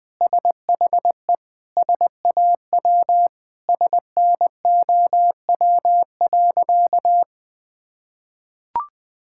Morse code, heard as a keyed tone, about 21 wpm, 700 Hz. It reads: SHE SAW SNOW.
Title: Sentences with 5-Word Phrases - 20wpm